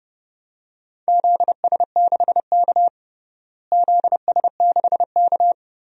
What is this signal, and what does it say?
Morse code, heard as a keyed tone, about 30 wpm, 700 Hz. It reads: ZS6K ZS6K
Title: DXCC-100 Call Signs - 4 Characters - Repeated 2x - 30wpm